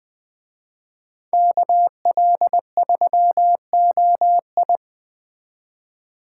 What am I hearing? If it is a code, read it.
Morse code, heard as a keyed tone, about 19 wpm, 700 Hz. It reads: KL3OI